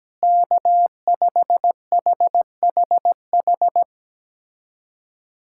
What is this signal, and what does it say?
Morse code, heard as a keyed tone, about 17 wpm, 700 Hz. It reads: K5HHH
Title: USA Callsign DE Callsign 17wpm